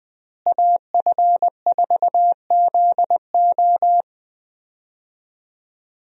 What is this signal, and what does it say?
Morse code, heard as a keyed tone, about 20 wpm, 700 Hz. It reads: AF4ZO
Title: USA Call Signs - 5 Characters - Repeated 3x - 20wpm